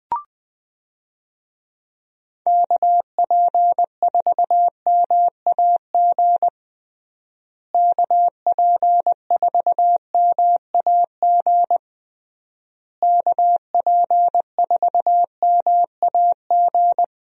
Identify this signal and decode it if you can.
Morse code, heard as a keyed tone, about 21 wpm, 700 Hz. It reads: KP4MAG KP4MAG KP4MAG
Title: USA Call Signs - 6 Characters - Repeated 3x - 20wpm